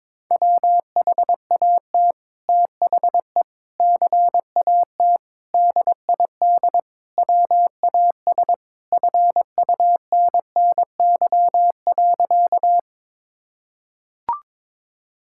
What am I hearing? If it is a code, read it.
Morse code, heard as a keyed tone, about 22 wpm, 700 Hz. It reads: WHAT THE CAT DID WAS FUNNY.
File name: practice / 22wpm / Sentences with 2-Word Phrases - 22wpm